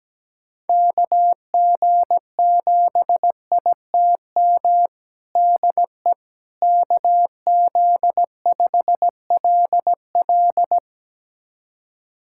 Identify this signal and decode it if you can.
Morse code, heard as a keyed tone, about 17 wpm, 700 Hz. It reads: KG7ITM DE KZ5LL
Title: USA Callsign DE Callsign 17wpm